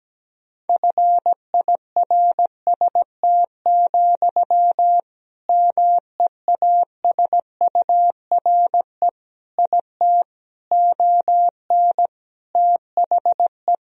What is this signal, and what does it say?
Morse code, heard as a keyed tone, about 17 wpm, 700 Hz. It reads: FIRST, MEASURE IT ON THE
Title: Sentences from Top 1000 Words - Review Part 3 - 17wpm